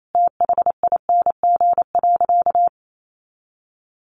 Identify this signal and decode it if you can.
Morse code, heard as a keyed tone, about 28 wpm, 700 Hz. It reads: THING.
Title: Sentences from Top 1400 Words 28wpm